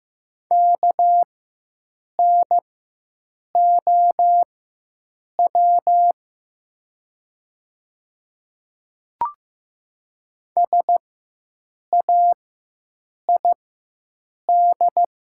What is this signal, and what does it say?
Morse code, heard as a keyed tone, about 15 wpm, spaced as for 7 wpm, 700 Hz. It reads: KNOW SAID
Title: 4x Character Spacing - Top 100 Words - 15wpm